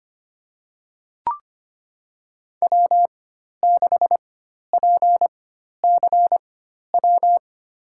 Morse code, heard as a keyed tone, about 25 wpm, spaced as for 12 wpm, 700 Hz. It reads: W6PCW
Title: USA Call Signs - 5 Characters - 4x Character Spacing - 25wpm